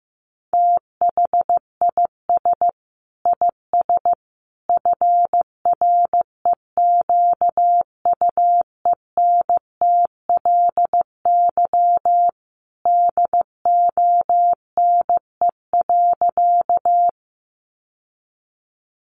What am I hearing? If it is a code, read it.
Morse code, heard as a keyed tone, about 15 wpm, 700 Hz. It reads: THIS IS FREQUENTLY DONE.
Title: Sentences from Top 1300 Words 15wpm